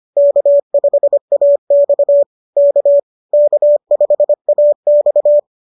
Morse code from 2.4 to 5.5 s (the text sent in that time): K K5AX